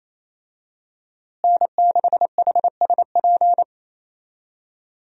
Morse code, heard as a keyed tone, about 28 wpm, 700 Hz. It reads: N6HSP
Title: USA Call Signs - 5 Characters - Repeated 2x - 28wpm